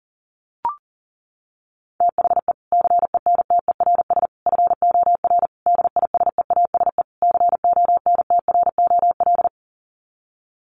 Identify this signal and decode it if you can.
Morse code, heard as a keyed tone, about 40 wpm, 700 Hz. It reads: THE CENTERS FOR DISEASE CONTROL
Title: Sets of 5 Words Encore2 40wpm